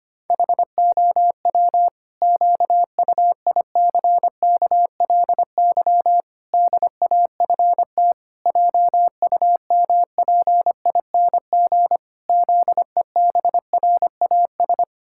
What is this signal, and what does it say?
Morse code, heard as a keyed tone, about 25 wpm, 700 Hz. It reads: HOW QUICKLY DAFT JUMPING ZEBRAS
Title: Pangram Challenge v2 - 25wpm